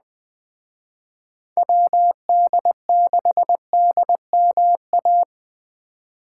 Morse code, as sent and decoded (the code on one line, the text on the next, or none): .-- -.. -.... -.. -- .-
WD6DMA